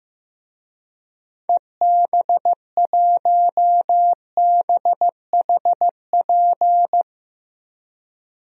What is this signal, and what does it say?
Morse code, heard as a keyed tone, about 15 wpm, 700 Hz. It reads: EB1BHP
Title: DXCC-100 Call Signs - 6 Characters - Repeated 2x - 15wpm